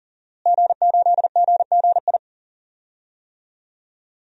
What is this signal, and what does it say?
Morse code, heard as a keyed tone, about 40 wpm, 700 Hz. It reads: G8GGI
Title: DXCC-100 Call Signs - 5 Characters - Repeated 2x - 40wpm